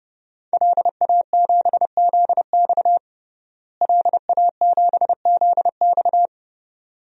Morse code, heard as a keyed tone, about 30 wpm, 700 Hz. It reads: LA7ZX LA7ZX
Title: DXCC-100 Call Signs - 5 Characters - Repeated 2x - 30wpm